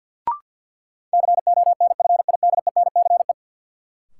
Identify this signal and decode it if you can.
Morse code, heard as a keyed tone, about 51 wpm, 700 Hz. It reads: CONFIDENCE
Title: Sentences from Top 1500 Words - New Word Drill Alt - 50wpm